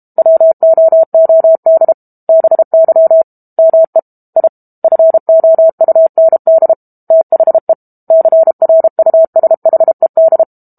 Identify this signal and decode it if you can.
Morse code, heard as a keyed tone, about 32 wpm, 650 Hz. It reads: WOOD BY ME I FOUND THE CRUSHED
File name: Book Two / 06-Chapter 6 - 32 WPM